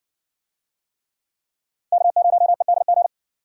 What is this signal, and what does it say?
Morse code, heard as a keyed tone, about 61 wpm, 700 Hz. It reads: K0EDG